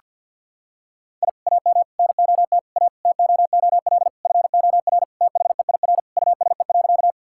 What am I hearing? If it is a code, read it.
Morse code, heard as a keyed tone, about 50 wpm, 700 Hz. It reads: I AM NOT A TOOL FOR THEIR USE.
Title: Sentences from Top 1000 Words - Review Part 6 - 50wpm